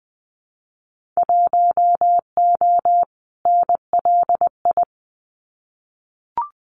Morse code, heard as a keyed tone, about 20 wpm, 700 Hz. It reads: 1O NLI
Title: Winter Field Day Exchanges 20wpm